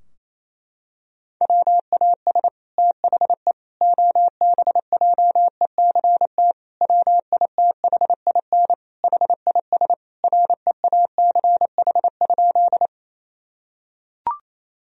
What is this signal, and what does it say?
Morse code, heard as a keyed tone, about 28 wpm, 700 Hz. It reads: WAS THE OBJECT WITHIN HIS REACH?
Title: Sentences from Top 1000 Words - Review Part 4 - 28wpm